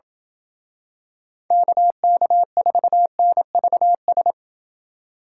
Morse code, heard as a keyed tone, about 28 wpm, 700 Hz. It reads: KK4NVS